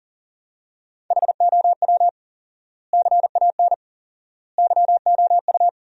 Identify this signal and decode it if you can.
Morse code, heard as a keyed tone, about 40 wpm, 700 Hz. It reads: HOW CAN YOU
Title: 4x Word-Spacing - Sentences from Top 200 Words - 40wpm